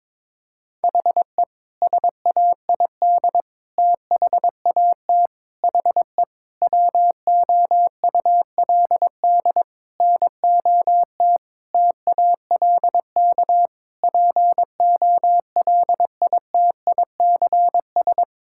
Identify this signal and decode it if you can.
Morse code, heard as a keyed tone, about 22 wpm, 700 Hz. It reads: HE SAID THAT HE WOULD NOT TALK POLITICS
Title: Sentences from Top 1000 Words - Review Part 6 - 22wpm